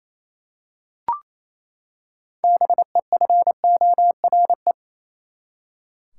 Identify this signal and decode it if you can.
Morse code, heard as a keyed tone, about 28 wpm, 700 Hz. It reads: BEFORE
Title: Six Letter Words - 28wpm